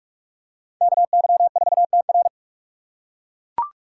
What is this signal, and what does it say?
Morse code, heard as a keyed tone, about 40 wpm, 700 Hz. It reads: KY4TR